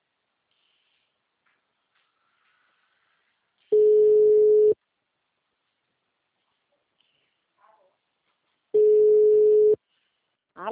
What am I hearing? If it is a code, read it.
Ringback tone (Europe, 425 Hz), with speech.